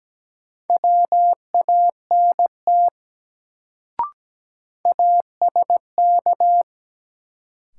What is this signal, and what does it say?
Morse code, heard as a keyed tone, about 17 wpm, 700 Hz. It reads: WANT ASK
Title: Sentences from Top 200 Words - New Word Drill - 17wpm